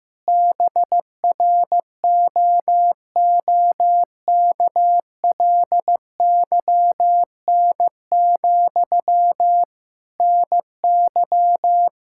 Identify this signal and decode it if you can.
Morse code, heard as a keyed tone, about 15 wpm, 700 Hz. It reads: BROOKLYN, NY
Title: Practice QSO - Exchange 1 - Beginning - 15wpm